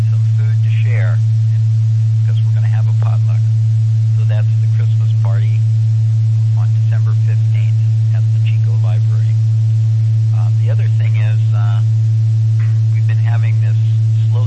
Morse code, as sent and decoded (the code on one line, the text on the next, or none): none